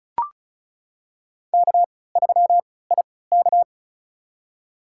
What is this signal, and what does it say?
Morse code, heard as a keyed tone, about 34 wpm, spaced as for 20 wpm, 700 Hz. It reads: K3IK
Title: USA Call Signs - 4 Characters - 3x Character Spacing - 35wpm